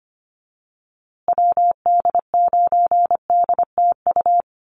Morse code, heard as a keyed tone, about 25 wpm, 700 Hz. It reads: WD9DTU
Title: USA Call Signs Repeated 3x 25wpm